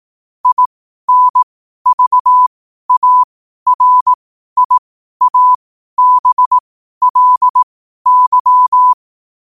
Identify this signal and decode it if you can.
Morse code, heard as a keyed tone, about 18 wpm, spaced as for 12 wpm, 1000 Hz. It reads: INVARIABLY
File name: qotdfiles / 12wpm